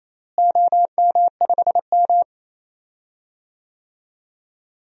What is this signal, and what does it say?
Morse code, heard as a keyed tone, about 28 wpm, 700 Hz. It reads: OM5M
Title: DXCC-100 Call Signs - 4 Characters - Repeated 2x - 28wpm